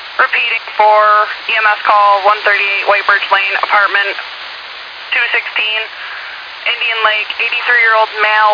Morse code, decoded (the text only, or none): none